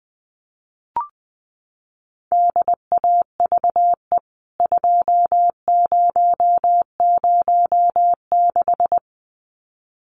Morse code, heard as a keyed tone, about 20 wpm, 700 Hz. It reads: DAVE 2006
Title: CWT Exchange 20wpm